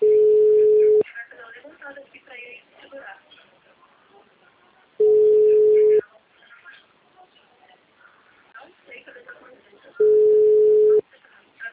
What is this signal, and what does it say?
Ringback tone (Europe, 425 Hz), with speech.